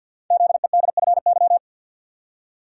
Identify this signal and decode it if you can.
Morse code, heard as a keyed tone, about 52 wpm, 700 Hz. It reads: ZEDFY
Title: Code Groups - 5 Letters - 50wpm